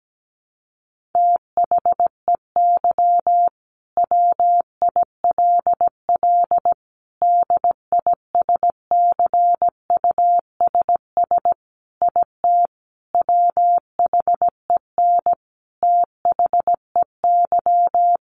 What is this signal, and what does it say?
Morse code, heard as a keyed tone, about 17 wpm, 700 Hz. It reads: THEY WILL DISCUSS IT WHEN THEY
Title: Sentences from Top 1400 Words 17wpm